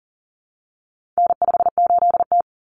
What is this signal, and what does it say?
Morse code, heard as a keyed tone, about 40 wpm, 700 Hz. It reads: N58T
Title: Common Cut Numbers - 4-Digit 40wpm